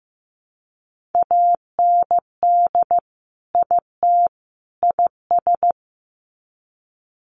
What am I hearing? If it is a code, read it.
Morse code, heard as a keyed tone, about 15 wpm, 700 Hz. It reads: AND IT IS
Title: Sets of 3 Words 15wpm